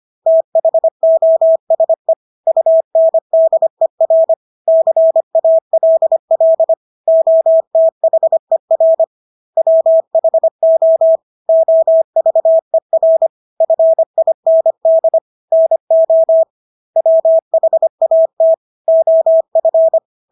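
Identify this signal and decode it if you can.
Morse code, heard as a keyed tone, about 25 wpm, 650 Hz. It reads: THOSE UNDER CALL OTHER WHO OVER FIND NO WHAT OF
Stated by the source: Morse practice of English words at 25 wpm from CW med Gnister.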